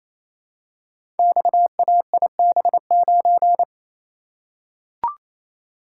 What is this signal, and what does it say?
Morse code, heard as a keyed tone, about 27 wpm, 700 Hz. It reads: XAIB9
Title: Code Groups - 5 Letter:Numbers - 28wpm